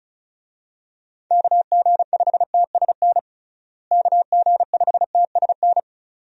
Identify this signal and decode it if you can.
Morse code, heard as a keyed tone, about 34 wpm, 700 Hz. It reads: KG5TSN KG5TSN
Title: USA Call Signs - 6 Characters - Repeated 2x - 35wpm